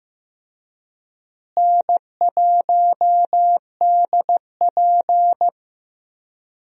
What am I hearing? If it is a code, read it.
Morse code, heard as a keyed tone, about 15 wpm, 700 Hz. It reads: N1DP
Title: USA Call Signs - 4 Characters - Repeated 2x - 15wpm